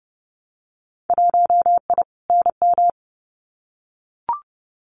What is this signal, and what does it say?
Morse code, heard as a keyed tone, about 30 wpm, 700 Hz. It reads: 1I NM
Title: Winter Field Day Exchanges Encore 30wpm